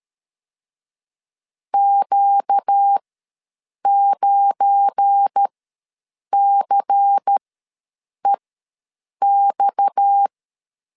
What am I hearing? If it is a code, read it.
Morse code, heard as a keyed tone, about 13 wpm, spaced as for 7 wpm, 800 Hz. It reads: Q9CEX
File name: LEZIONI / 7 GF9Z / 7 GF9 ws 12wpm 3344